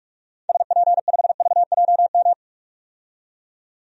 Morse code, heard as a keyed tone, about 42 wpm, 700 Hz. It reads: SP5VJM
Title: DXCC-100 Call Signs - 6 Characters - 45wpm